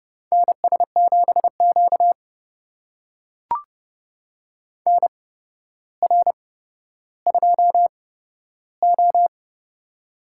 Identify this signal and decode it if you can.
Morse code, heard as a keyed tone, about 30 wpm, spaced as for 13 wpm, 700 Hz. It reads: NS7Q NR2O
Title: USA Call Signs - 4 Characters - 8x Character Spacing - 30wpm